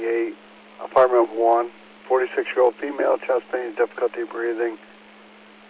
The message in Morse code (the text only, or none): none